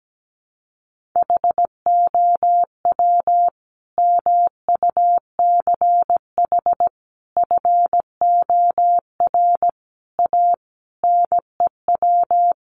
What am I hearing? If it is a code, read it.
Morse code, heard as a keyed tone, about 17 wpm, 700 Hz. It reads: HOW MUCH FOR A NEW
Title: Sentences from Top 1500 Words 17wpm